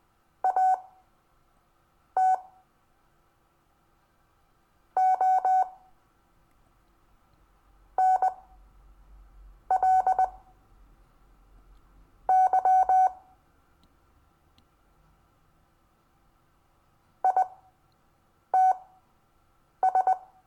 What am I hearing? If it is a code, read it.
Morse code, heard as a keyed tone, about 20 wpm, spaced as for 5 wpm, 750 Hz. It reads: AT ONLY ITS
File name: cw / Lesson22 REVIEW